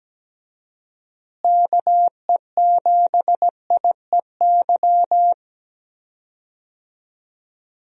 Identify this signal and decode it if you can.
Morse code, heard as a keyed tone, about 17 wpm, 700 Hz. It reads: KE7IEY
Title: USA Call Signs - 6 Characters - Repeated 3x - 17wpm